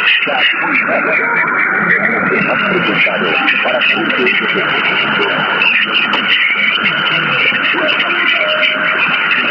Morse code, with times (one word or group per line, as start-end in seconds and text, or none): none